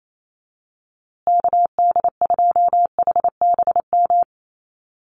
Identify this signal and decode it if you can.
Morse code, heard as a keyed tone, about 28 wpm, 700 Hz. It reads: KD2HBM
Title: USA Call Signs Repeated 3x 28wpm